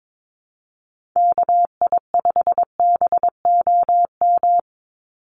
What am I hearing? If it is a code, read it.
Morse code, heard as a keyed tone, about 22 wpm, 700 Hz. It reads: KI5BOM
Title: USA Call Signs 22wpm